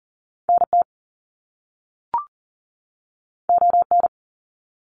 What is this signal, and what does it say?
Morse code, heard as a keyed tone, about 40 wpm, 700 Hz. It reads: NT ON
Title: Canadian Provinces and Territories 40wpm